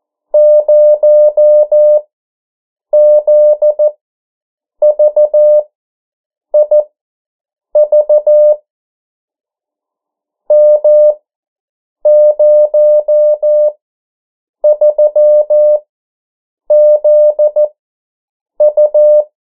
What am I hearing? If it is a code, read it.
Morse code, heard as a keyed tone, about 14 wpm, spaced as for 7 wpm, 600 Hz. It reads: 0ZVIV M03ZU
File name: morse mp3 / 09 urok 8